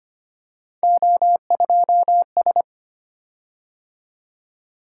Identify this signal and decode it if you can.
Morse code, heard as a keyed tone, about 25 wpm, 700 Hz. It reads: O2S
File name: Three Letter-Number Combinations - 25wpm